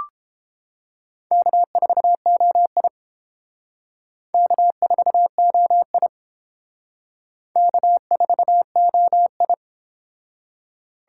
Morse code, heard as a keyed tone, about 29 wpm, 700 Hz. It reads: K4OI K4OI K4OI